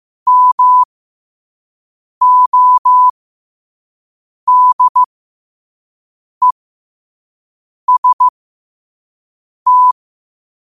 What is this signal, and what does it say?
Morse code, heard as a keyed tone, about 15 wpm, spaced as for 5 wpm, 1000 Hz. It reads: MODEST